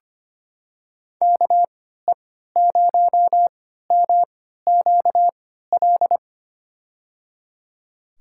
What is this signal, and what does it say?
Morse code, heard as a keyed tone, about 25 wpm, spaced as for 14 wpm, 700 Hz. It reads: KE0MQL
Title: USA Call Signs - 6 Characters - 3x Character Spacing - 25wpm